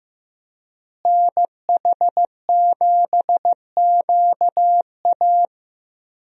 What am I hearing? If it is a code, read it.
Morse code, heard as a keyed tone, about 15 wpm, 700 Hz. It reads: NH7QA
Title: USA Call Signs - 5 Characters - Repeated 2x - 15wpm